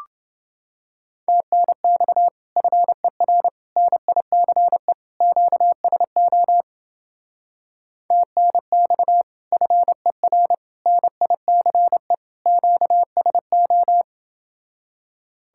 Morse code, heard as a keyed tone, about 29 wpm, 700 Hz. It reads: TNX FER NICE QSO TNX FER NICE QSO